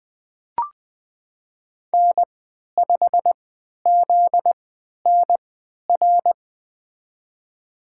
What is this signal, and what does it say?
Morse code, heard as a keyed tone, about 20 wpm, spaced as for 11 wpm, 700 Hz. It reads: N5ZNR
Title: USA Call Signs - 5 Characters - 3x Character Spacing - 20wpm